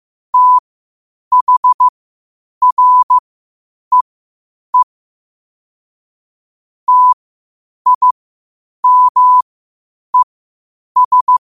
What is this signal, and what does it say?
Morse code, heard as a keyed tone, about 15 wpm, spaced as for 8 wpm, 1000 Hz. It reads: THREE TIMES
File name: qotdfiles / 8wpm